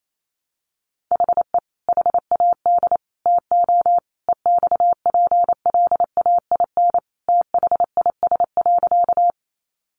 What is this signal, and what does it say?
Morse code, heard as a keyed tone, about 28 wpm, 700 Hz. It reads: HE HAD TO EXPLAIN THIS.
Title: Sentences from Top 1100 Words 28wpm